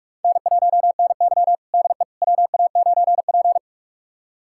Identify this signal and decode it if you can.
Morse code, heard as a keyed tone, about 44 wpm, 700 Hz. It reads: N1NY DE WA9P
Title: USA Callsign DE Callsign - 4 Characters - 45wpm